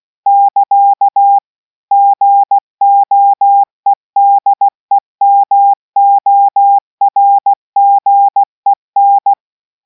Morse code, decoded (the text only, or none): <KA> GOEDEMORGEN